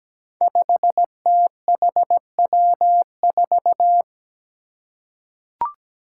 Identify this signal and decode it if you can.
Morse code, heard as a keyed tone, about 17 wpm, 700 Hz. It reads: 5THW4
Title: Five Challenging Letter-Number Combinations 17wpm